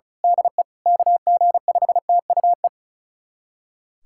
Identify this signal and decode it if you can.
Morse code, heard as a keyed tone, about 36 wpm, 700 Hz. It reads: DE KG5TUE
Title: USA Callsign DE Callsign 35wpm